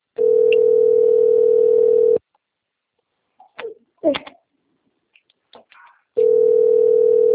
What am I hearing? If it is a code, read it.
Ringback tone (North America), 440 + 480 Hz.